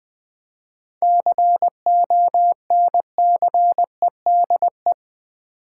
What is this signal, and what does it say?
Morse code, heard as a keyed tone, about 20 wpm, 700 Hz. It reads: CONCEDE
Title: 7 Letter Words - First 3 Letters Given - 20wpm